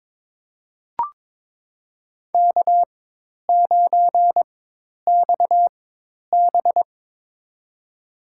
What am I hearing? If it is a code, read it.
Morse code, heard as a keyed tone, about 22 wpm, spaced as for 10 wpm, 700 Hz. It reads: K9XB
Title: USA Call Signs - 4 Characters - 4x Character Spacing - 22wpm